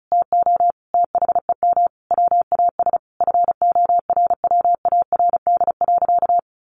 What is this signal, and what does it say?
Morse code, heard as a keyed tone, about 34 wpm, 700 Hz. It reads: TO THEM WAS FORWARD.
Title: Sentences from Top 1500 Words 35wpm